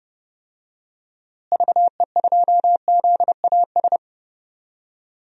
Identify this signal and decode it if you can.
Morse code, heard as a keyed tone, about 30 wpm, 700 Hz. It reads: VE2ZAS
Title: DXCC-100 Call Signs - 6 Characters - Repeated 2x - 30wpm